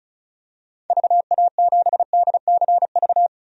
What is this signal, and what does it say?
Morse code, heard as a keyed tone, about 36 wpm, 690 Hz. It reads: VA7DCV
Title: DXCC-100 Call Signs - 6 Characters - 35wpm